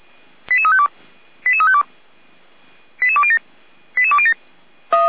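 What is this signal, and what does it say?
Five-tone selective call (ZVEI), heard as quick runs of five tones, tones of about 80 ms. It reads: 88352 88352 88207 88207
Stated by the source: ZVEI five-tone selective call received on an SDR, fire brigade feuerwehr-orlamuende.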